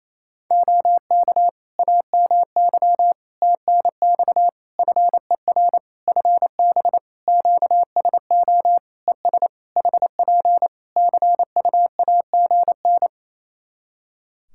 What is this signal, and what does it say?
Morse code, heard as a keyed tone, about 27 wpm, 700 Hz. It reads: OK AMY TNX FER FB QSO ES HP CUAGN
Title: Practice QSO - Exchange 4 - Ending - 28wpm